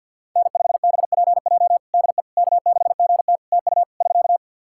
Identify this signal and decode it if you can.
Morse code, heard as a keyed tone, about 51 wpm, 700 Hz. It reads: N5BPJ DE K6GT TU <SK>
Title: K1USN SST QSO Practice 50wpm